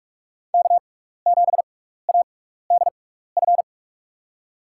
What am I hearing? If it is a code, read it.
Morse code, heard as a keyed tone, about 47 wpm, spaced as for 16 wpm, 700 Hz. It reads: K7ADF
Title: USA Call Signs - 5 Characters - 6x Character Spacing - 45wpm